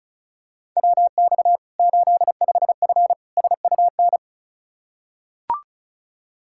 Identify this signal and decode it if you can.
Morse code, heard as a keyed tone, about 34 wpm, 700 Hz. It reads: WX 85F SUN
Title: Practice QSO - Exchange 2 - Info - 35wpm